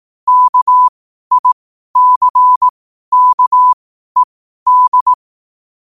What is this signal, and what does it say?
Morse code, heard as a keyed tone, about 18 wpm, spaced as for 12 wpm, 1000 Hz. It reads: KICKED
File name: qotdfiles / 12wpm